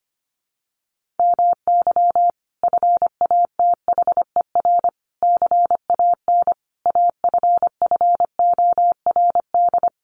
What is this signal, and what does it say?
Morse code, heard as a keyed tone, about 25 wpm, 700 Hz. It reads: MY FATHER CAN AFFORD
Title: Sentences from Top 1500 Words 25wpm